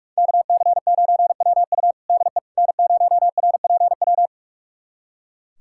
Morse code, heard as a keyed tone, about 41 wpm, 700 Hz. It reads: KC9WU DE N0RPW
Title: USA Callsign DE Callsign - 5 Characters - 45wpm